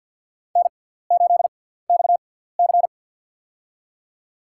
Morse code, heard as a keyed tone, about 47 wpm, spaced as for 17 wpm, 700 Hz. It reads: N8XC
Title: USA Call Signs - 4 Characters - 6x Character Spacing - 50wpm